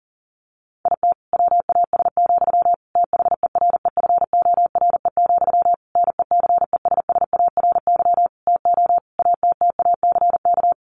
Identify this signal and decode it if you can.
Morse code, heard as a keyed tone, about 40 wpm, 700 Hz. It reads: IT WAS, THEREFORE, NECESSARY TO ATTACK
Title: Sentences from Top 1500 Words 40wpm